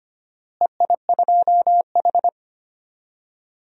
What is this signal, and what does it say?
Morse code, heard as a keyed tone, about 25 wpm, 700 Hz. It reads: EI2H
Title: DXCC-100 Call Signs - 4 Characters - Repeated 2x - 25wpm